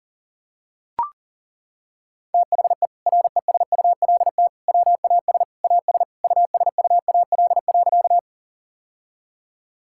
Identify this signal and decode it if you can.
Morse code, heard as a keyed tone, about 40 wpm, 690 Hz. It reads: THE RESULT WAS AS USUAL.
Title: Sentences from Top 1000 Words - Review Part 5 - 40wpm